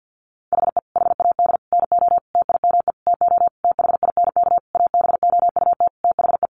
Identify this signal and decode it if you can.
Morse code, heard as a keyed tone, about 50 wpm, 700 Hz. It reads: HE HAD NO TIME TO THINK ABOUT THE
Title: Sentences from Top 700 Words 50wpm